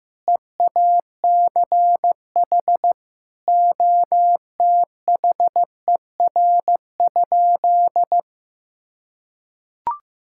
Morse code, heard as a keyed tone, about 15 wpm, 700 Hz. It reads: EACH OTHER?
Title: Sentences from Top 1000 Words - Review Part 6 - 15wpm